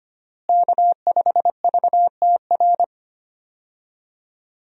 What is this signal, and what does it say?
Morse code, heard as a keyed tone, about 25 wpm, 700 Hz. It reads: K5VTR
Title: USA Call Signs - 5 Characters - Repeated 2x - 25wpm